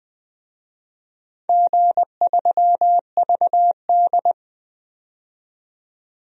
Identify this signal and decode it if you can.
Morse code, heard as a keyed tone, about 20 wpm, 700 Hz. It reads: G3VD
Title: DXCC-100 Call Signs - 4 Characters - 20wpm